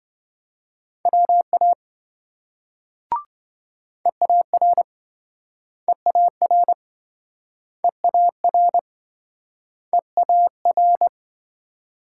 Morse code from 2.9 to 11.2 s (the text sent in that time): EAR EAR EAR EAR